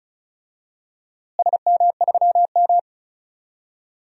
Morse code, heard as a keyed tone, about 34 wpm, 700 Hz. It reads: SM3M
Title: DXCC-100 Call Signs - 4 Characters - Repeated 3x - 35wpm